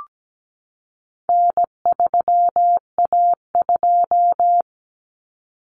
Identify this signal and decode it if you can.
Morse code, heard as a keyed tone, about 17 wpm, 700 Hz. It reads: N3A2
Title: Common Cut Numbers - 4-Digit 17wpm